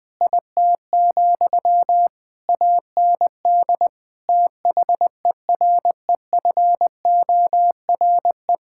Morse code, heard as a keyed tone, about 20 wpm, 700 Hz. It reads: IT, AND THEREFORE
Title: Sentences from Top 1000 Words - Review Part 5 - 20wpm